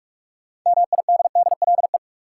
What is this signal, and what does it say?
Morse code, heard as a keyed tone, about 47 wpm, 700 Hz. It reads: MIDDLE
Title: Sentences from Top 900 Words - New Word Drill - 45wpm